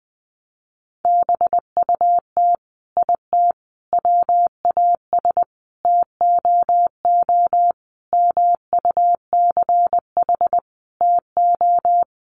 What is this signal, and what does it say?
Morse code, heard as a keyed tone, about 19 wpm, 700 Hz. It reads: BUT IT WAS TOO MUCH TO
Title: Sentences from Top 1100 Words 20wpm